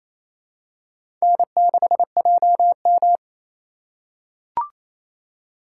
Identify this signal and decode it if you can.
Morse code, heard as a keyed tone, about 28 wpm, 700 Hz. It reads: N6JM
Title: USA Call Signs - 4 Characters - 2x Character Spacing - 28wpm